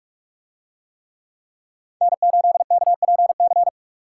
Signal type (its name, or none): Morse code